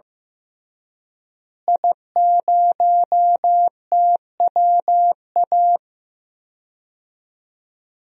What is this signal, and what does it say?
Morse code, heard as a keyed tone, about 15 wpm, 700 Hz. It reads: I0TWA
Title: DXCC-100 Call Signs - 5 Characters - Repeated 3x - 15wpm